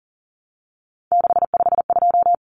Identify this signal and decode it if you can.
Morse code, heard as a keyed tone, about 40 wpm, 700 Hz. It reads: 652
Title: Common Cut Numbers - 3-Digit 40wpm